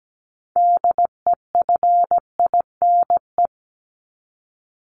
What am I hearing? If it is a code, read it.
Morse code, heard as a keyed tone, about 17 wpm, 700 Hz. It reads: DEFINE
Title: ED Suffix - Top 100 Words with Base Words 17wpm